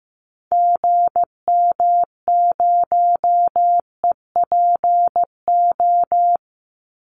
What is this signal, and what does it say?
Morse code, heard as a keyed tone, about 15 wpm, 700 Hz. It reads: GM0EPO
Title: Call Signs Repeated 2x - 15wpm